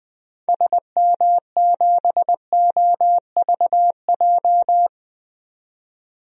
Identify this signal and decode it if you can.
Morse code, heard as a keyed tone, about 20 wpm, 700 Hz. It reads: SM7OVJ
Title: DXCC-100 Call Signs - 6 Characters - 20wpm